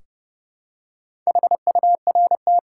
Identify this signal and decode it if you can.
Morse code, heard as a keyed tone, about 30 wpm, 700 Hz. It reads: HURT
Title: Four Letter Words - 30wpm